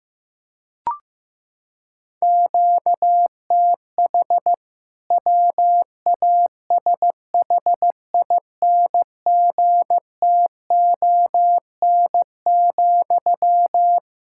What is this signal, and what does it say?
Morse code, heard as a keyed tone, about 15 wpm, 700 Hz. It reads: QTH WASHINGTON,
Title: K1USN SST QSO Practice 15wpm